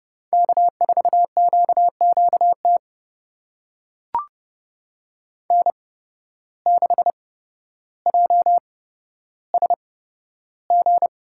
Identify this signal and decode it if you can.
Morse code, heard as a keyed tone, about 31 wpm, spaced as for 13 wpm, 700 Hz. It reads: K4QQT N6JSG
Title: USA Call Signs - 5 Characters - 8x Character Spacing - 30wpm